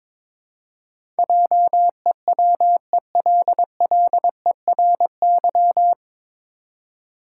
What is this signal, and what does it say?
Morse code, heard as a keyed tone, about 21 wpm, 700 Hz. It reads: JEWELLERY
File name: 9 Letter Words - First 3 Letters Given - 22wpm